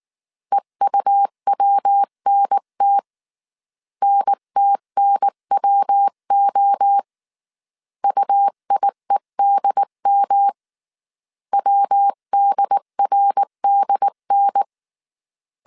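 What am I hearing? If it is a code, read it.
Morse code, heard as a keyed tone, about 19 wpm, 800 Hz. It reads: EUWNT NTNWO UIEDM WDRDN
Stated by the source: Morse code dictation at 18 wpm from the IW3SGT QRS Morse code training course, five-character groups of 300 characters.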